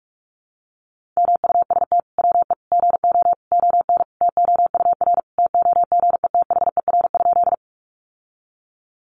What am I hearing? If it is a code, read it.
Morse code, heard as a keyed tone, about 44 wpm, 700 Hz. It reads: MUST WE GO ON TOUR TOGETHER?
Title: Sentences from Top 1400 Words 45wpm